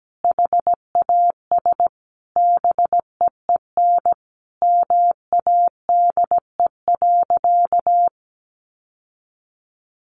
Morse code, heard as a keyed tone, about 17 wpm, 700 Hz. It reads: HAS BEEN MADE.
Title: Sentences from Top 1500 Words 17wpm